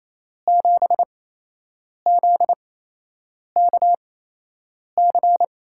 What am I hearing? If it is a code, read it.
Morse code, heard as a keyed tone, about 28 wpm, spaced as for 8 wpm, 700 Hz. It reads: 7ZKC
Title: USA Call Signs - 5 Characters - 8x Character Spacing - 28wpm